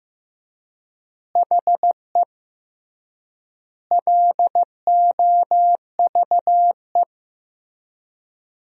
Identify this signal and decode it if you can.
Morse code, heard as a keyed tone, about 14 wpm, 700 Hz. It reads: HE LOVE
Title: 3x Word-Spacing - Sentences from Top 400 Words - 15wpm